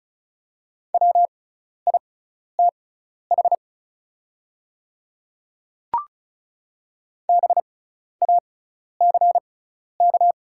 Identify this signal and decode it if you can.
Morse code, heard as a keyed tone, about 36 wpm, spaced as for 12 wpm, 690 Hz. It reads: WITH BACK
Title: 6x Character Spacing - Top 500 Words - 35wpm